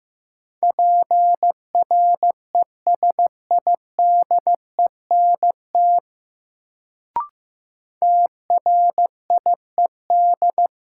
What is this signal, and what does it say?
Morse code, heard as a keyed tone, about 16 wpm, 700 Hz. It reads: PRESIDENT TRIED
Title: Sets of 2 Words - New Word Drill - 15wpm